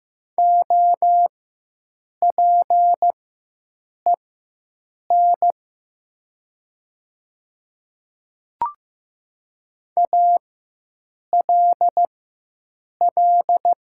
Morse code, heard as a keyed tone, about 15 wpm, spaced as for 7 wpm, 700 Hz. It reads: OPEN ALL
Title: 4x Character Spacing - Top 300 Words - 15wpm